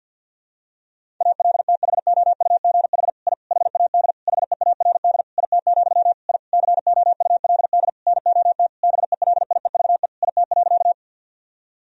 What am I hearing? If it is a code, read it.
Morse code, heard as a keyed tone, about 47 wpm, 700 Hz. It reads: ALTHOUGH I HAD HEARD IT, I COULD NOT BELIEVE IT.
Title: Sentences from Top 1000 Words - Review Part 3 - 50wpm